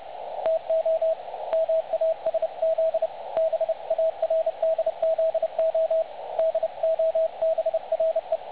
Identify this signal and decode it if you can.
Morse code, heard as a keyed tone, about 30 wpm, 660 Hz, with radio noise. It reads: TO MASZ BARDZO DOBRE
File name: dlazaawansowanych / nagraniazpasma / websdr recording 3565.0kHz